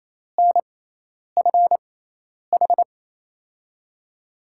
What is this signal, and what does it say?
Morse code, heard as a keyed tone, about 28 wpm, spaced as for 10 wpm, 700 Hz. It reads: NFH
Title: USA Call Signs - 5 Characters - 6x Character Spacing - 28wpm